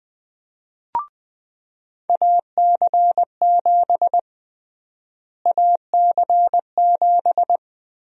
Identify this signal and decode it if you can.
Morse code, heard as a keyed tone, about 20 wpm, 700 Hz. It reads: AC7 AC7
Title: USA Call Sign Prefixes - Repeated 2x - 20wpm